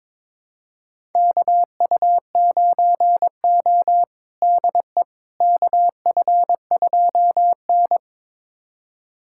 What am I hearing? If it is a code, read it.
Morse code, heard as a keyed tone, about 21 wpm, 700 Hz. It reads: KU9O DE KF2N
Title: USA Callsign DE Callsign - 4 Characters - 22wpm